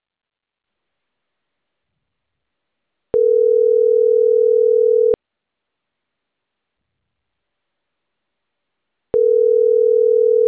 Ringback tone (North America), 440 + 480 Hz.